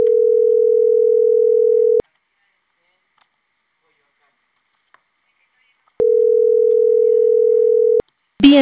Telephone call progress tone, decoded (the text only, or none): ringback tone